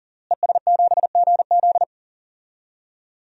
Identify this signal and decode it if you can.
Morse code, heard as a keyed tone, about 38 wpm, 700 Hz. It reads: ES7GZ
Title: DXCC-100 Call Signs - 5 Characters - Repeated 2x - 40wpm